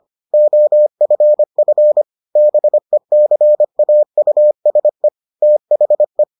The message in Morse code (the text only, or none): OFF BECAUSE THE